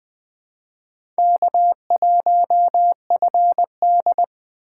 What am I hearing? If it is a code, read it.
Morse code, heard as a keyed tone, about 21 wpm, 700 Hz. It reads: K1FD
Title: Speed Racing - USA 1x2 Call Signs 17wpm